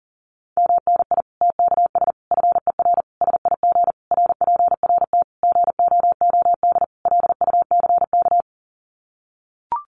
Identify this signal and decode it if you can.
Morse code, heard as a keyed tone, about 40 wpm, 700 Hz. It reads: MNI TKS FER SIG RPRT GOOD LUCK
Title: Common QSO Phrases 40wpm